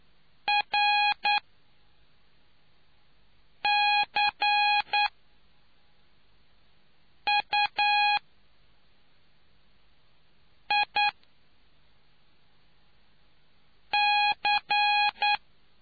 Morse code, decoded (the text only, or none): RCUIC